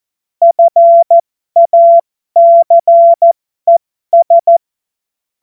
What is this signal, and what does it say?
Morse code, heard as a keyed tone, about 14 wpm, spaced as for 12 wpm, 680 Hz. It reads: FACES
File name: Morsekurs / Witze